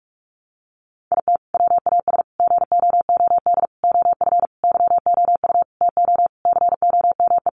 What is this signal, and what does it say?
Morse code, heard as a keyed tone, about 44 wpm, 700 Hz. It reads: IT WAS GOOD OF YOU TO COME
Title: Sentences from Top 1100 Words 45wpm